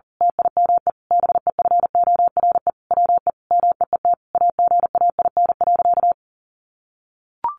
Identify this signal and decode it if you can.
Morse code, heard as a keyed tone, about 40 wpm, 700 Hz. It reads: TIME BEFORE WE MEET AGAIN.
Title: Sentences from Top 1100 Words 40wpm